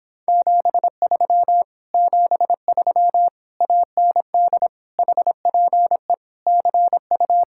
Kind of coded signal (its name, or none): Morse code